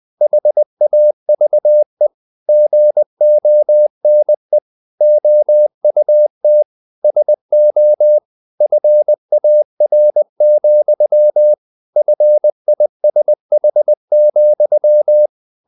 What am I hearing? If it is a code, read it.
Morse code, heard as a keyed tone, about 20 wpm, 600 Hz. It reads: HAVE GONE OUT SO FAR, FISH,
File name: the old man and the sea part 2 20wpm